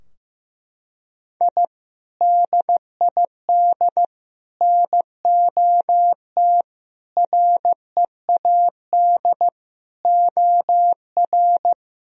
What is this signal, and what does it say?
Morse code, heard as a keyed tone, about 14 wpm, 700 Hz. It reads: I DID NOT READ OR
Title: Sentences from Top 1000 Words - Review Part 3 - 15wpm